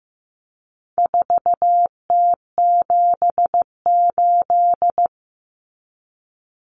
Morse code, heard as a keyed tone, about 15 wpm, 700 Hz. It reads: 4T78
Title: Common Cut Numbers - 4-Digit 15wpm